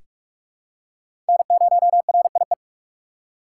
Morse code, heard as a keyed tone, about 41 wpm, 700 Hz. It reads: N0RIE
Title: USA Call Signs - 5 Characters - 2x Character Spacing - 45wpm